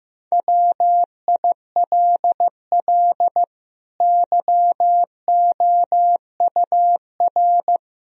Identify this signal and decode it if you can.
Morse code, heard as a keyed tone, about 15 wpm, 700 Hz. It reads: WILL YOUR
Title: Sentences from Top 1000 Words - Review Part 3 - 15wpm